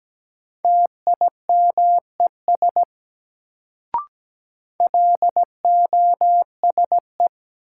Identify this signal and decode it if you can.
Morse code, heard as a keyed tone, about 17 wpm, 700 Hz. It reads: TIMES LOSE
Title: Sentences from Top 1100 Words - New Word Drill Alt - 17wpm